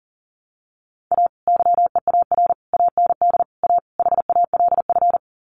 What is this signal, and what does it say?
Morse code, heard as a keyed tone, about 40 wpm, 700 Hz. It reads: A YEAR AND A HALF
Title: Sets of 5 Words 40wpm